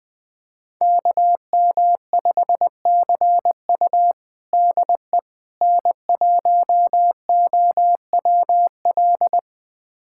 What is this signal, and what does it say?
Morse code, heard as a keyed tone, about 20 wpm, 700 Hz. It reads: KM5CU DE N1OWL
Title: USA Callsign DE Callsign - 5 Characters - 20wpm